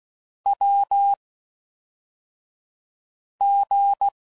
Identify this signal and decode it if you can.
Morse code, heard as a keyed tone, about 16 wpm, 800 Hz. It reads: W G